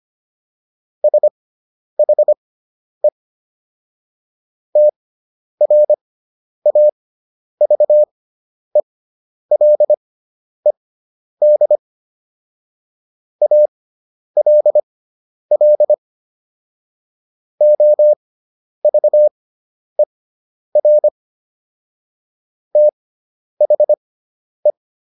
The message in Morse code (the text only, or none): SHE TRAVELED ALL OVER THE